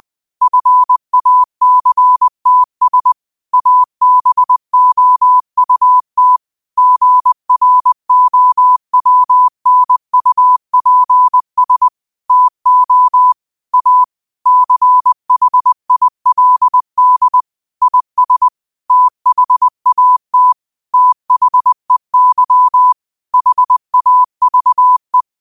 Morse code, heard as a keyed tone, about 20 wpm, 1000 Hz. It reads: FACTS ABOUT GROWNUPS TO A CHILD IS THAT THEY HAVE
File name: qotdfiles / 20wpm